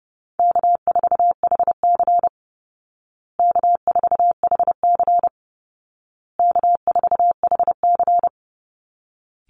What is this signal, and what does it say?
Morse code, heard as a keyed tone, about 30 wpm, 700 Hz. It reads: K4HC K4HC K4HC
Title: Easy Call Signs Repeated 3x - 30wpm